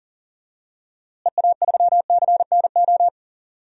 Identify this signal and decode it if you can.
Morse code, heard as a keyed tone, about 40 wpm, 700 Hz. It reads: EA3CNO
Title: DXCC-100 Call Signs - 6 Characters - Repeated 2x - 40wpm